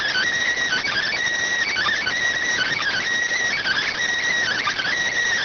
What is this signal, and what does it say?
Emergency Alert System (SAME) tones (header bursts), heard as harsh buzzing bursts. It is heard from 0.6 to 4.9 s.